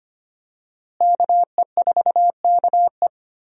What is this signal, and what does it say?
Morse code, heard as a keyed tone, about 25 wpm, 700 Hz. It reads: KE4KE
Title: USA Call Signs - 5 Characters - Repeated 2x - 25wpm